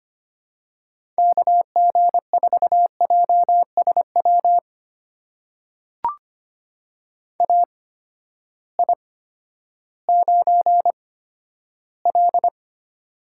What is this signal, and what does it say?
Morse code, heard as a keyed tone, about 25 wpm, spaced as for 12 wpm, 700 Hz. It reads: KG4JSW AI9L
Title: USA Call Signs - 8x Character Spacing - 25wpm